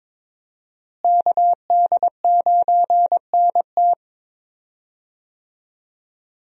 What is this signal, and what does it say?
Morse code, heard as a keyed tone, about 22 wpm, 700 Hz. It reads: KD9NT
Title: USA Call Signs - 5 Characters - Repeated 2x - 22wpm